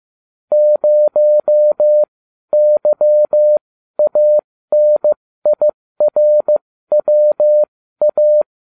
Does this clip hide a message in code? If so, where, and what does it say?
Yes: Morse code from 0.5 to 8.4 s, heard as a keyed tone, about 15 wpm, spaced as for 12 wpm, 600 Hz. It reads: 0YANIRWA